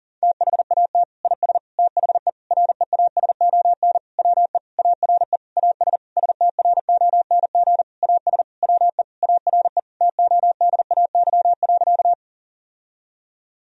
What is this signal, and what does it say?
Morse code, heard as a keyed tone, about 40 wpm, 700 Hz. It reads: THAT IS THE REASON WE ARE AS STRONG AS WE ARE TODAY.
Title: Sentences from Top 1000 Words - Review Part 4 - 40wpm